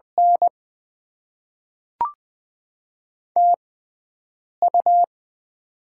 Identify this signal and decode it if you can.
Morse code, heard as a keyed tone, about 20 wpm, spaced as for 7 wpm, 690 Hz. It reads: N TU
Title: Top 20 QSO Elements - 6x Character Spacing - 20wpm